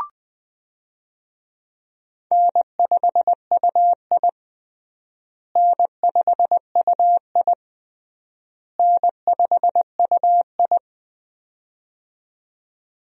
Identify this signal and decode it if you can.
Morse code, heard as a keyed tone, about 20 wpm, 700 Hz. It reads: N5UI N5UI N5UI
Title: USA Call Signs - 4 Characters - Repeated 3x - 20wpm